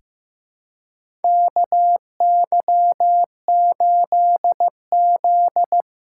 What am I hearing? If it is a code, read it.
Morse code, heard as a keyed tone, about 16 wpm, 700 Hz. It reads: KY8Z
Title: USA Callsign DE Callsign - 4 Characters - 15wpm